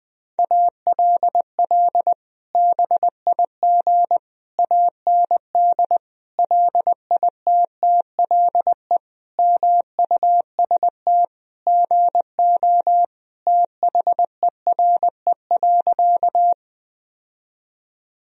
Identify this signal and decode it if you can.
Morse code, heard as a keyed tone, about 20 wpm, 700 Hz. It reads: ALL BIG AND LITTLE MUST GO THERE.
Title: Sentences from Top 1000 Words - Review Part 1 - 20wpm